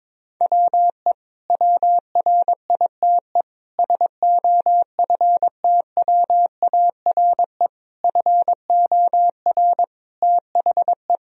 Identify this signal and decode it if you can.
Morse code, heard as a keyed tone, about 22 wpm, 700 Hz. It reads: WE WRITE SOFTWARE FOR THE
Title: Sentences from Top 1000 Words - Review Part 3 - 22wpm